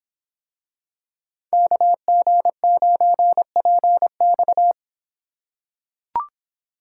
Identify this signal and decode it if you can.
Morse code, heard as a keyed tone, about 26 wpm, 700 Hz. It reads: KG9PX